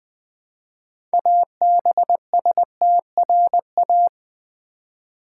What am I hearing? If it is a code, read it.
Morse code, heard as a keyed tone, about 20 wpm, 700 Hz. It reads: ABSTRA
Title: Build A Word - 2 Letters At A Time - 8-Letter Words - 20wpm